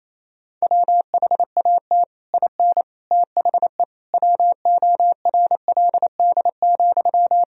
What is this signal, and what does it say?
Morse code, heard as a keyed tone, about 28 wpm, 700 Hz. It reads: WHAT IN THE WORLD,
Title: Sentences from Top 1000 Words - Review Part 1 - 28wpm